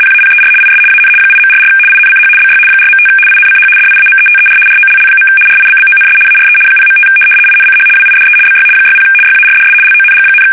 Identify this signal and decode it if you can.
RTTY, about 75 baud. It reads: PAPER IS AN 8.44 INCH BY 4.5 INCH DIAMETER ROLL. RIBBONS ARE 0.5 INCH WIDE BY 60 YARDS LONG,
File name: fsk850-75B / teletype-corp